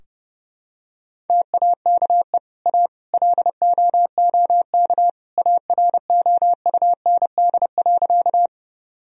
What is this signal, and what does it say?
Morse code, heard as a keyed tone, about 30 wpm, 700 Hz. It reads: TAKE A LOOK AROUND.
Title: Sentences from Top 1000 Words - Review Part 3 - 30wpm